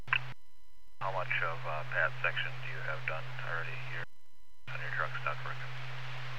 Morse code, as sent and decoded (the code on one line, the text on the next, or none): none